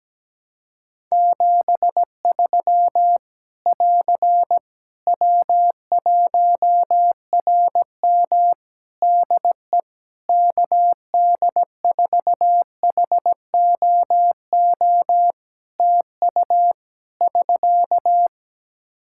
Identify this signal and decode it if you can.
Morse code, heard as a keyed tone, about 17 wpm, 700 Hz. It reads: 73 <AR> W1RM DE KD4HOO TU <SK>
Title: Practice QSO - Exchange 4 - Ending - 17wpm